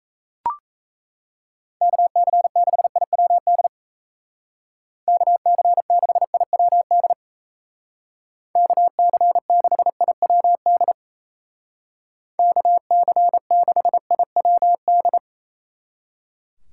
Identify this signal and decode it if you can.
Morse code, heard as a keyed tone, about 35 wpm, 700 Hz. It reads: KC6IWD KC6IWD KC6IWD KC6IWD